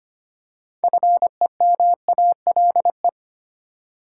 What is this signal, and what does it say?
Morse code, heard as a keyed tone, about 25 wpm, 700 Hz. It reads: FEMALE
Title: Six Letter Words - 25wpm